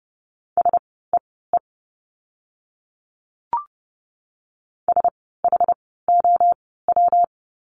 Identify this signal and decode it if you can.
Morse code, heard as a keyed tone, about 30 wpm, spaced as for 17 wpm, 700 Hz. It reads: SEE SHOW
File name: course / 30:10wpm / 230 Lesson 37f - Letter x - Words 30wpm